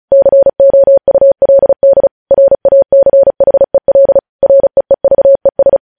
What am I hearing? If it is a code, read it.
Morse code, heard as a keyed tone, about 36 wpm, 550 Hz. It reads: COULD RACHEL REEVES
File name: podcast / News headlines-1x-35x35